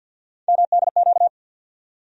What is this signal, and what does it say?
Morse code, heard as a keyed tone, about 53 wpm, 700 Hz. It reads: MDQ
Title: USA Call Sign Postfixes - 3 Letters 50wpm